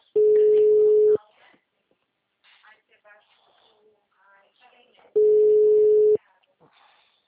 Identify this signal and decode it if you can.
Ringback tone (Europe, 425 Hz), with speech.